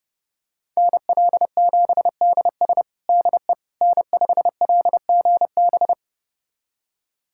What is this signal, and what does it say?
Morse code, heard as a keyed tone, about 30 wpm, 700 Hz. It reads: NL7DS DE N5LGB
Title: USA Callsign DE Callsign - 5 Characters - 30wpm